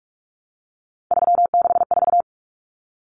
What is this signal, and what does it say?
Morse code, heard as a keyed tone, about 44 wpm, 700 Hz. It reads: 364
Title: Numbers - 3-Digit 45wpm